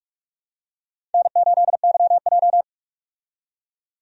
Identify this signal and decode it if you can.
Morse code, heard as a keyed tone, about 45 wpm, 700 Hz. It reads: N8YJ
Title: USA Call Signs - 4 Characters - 45wpm